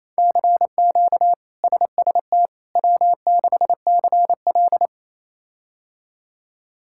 Morse code, heard as a keyed tone, about 28 wpm, 700 Hz. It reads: CQ SST W6CL
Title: K1USN SST QSO Practice - 28wpm